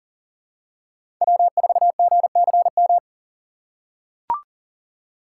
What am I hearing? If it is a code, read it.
Morse code, heard as a keyed tone, about 40 wpm, 700 Hz. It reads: W4GCM
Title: USA Call Signs - 4x Character Spacing - 40wpm